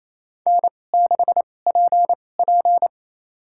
Morse code, heard as a keyed tone, about 28 wpm, spaced as for 20 wpm, 700 Hz. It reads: N6PP
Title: USA Call Signs - 4 Characters - 2x Character Spacing - 28wpm